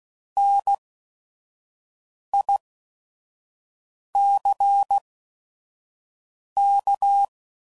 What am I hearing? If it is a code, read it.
Morse code, heard as a keyed tone, about 16 wpm, spaced as for 5 wpm, 800 Hz. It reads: NICK